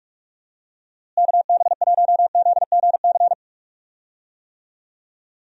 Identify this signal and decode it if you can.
Morse code, heard as a keyed tone, about 45 wpm, 700 Hz. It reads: KB1ZGC
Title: USA Call Signs - 6 Characters - Repeated 2x - 45wpm